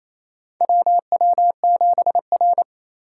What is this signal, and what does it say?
Morse code, heard as a keyed tone, about 27 wpm, 700 Hz. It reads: WW7R
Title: USA Call Signs - 4 Characters - 3x Character Spacing - 28wpm